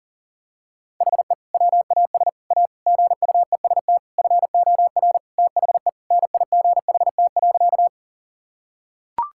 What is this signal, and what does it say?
Morse code, heard as a keyed tone, about 37 wpm, 700 Hz. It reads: HE WAS A GUEST FOR THE NIGHT.
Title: Sentences from Top 1600 Words 40wpm